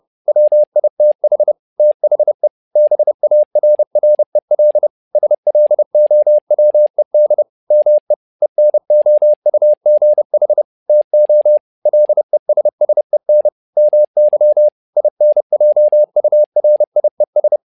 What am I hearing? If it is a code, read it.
Morse code, heard as a keyed tone, about 30 wpm, 600 Hz. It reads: WITH THE BARREL SLOWED ME ENOUGH TO LESSEN MY INJURIES
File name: learning center / tower trouble / tower trouble 30wpm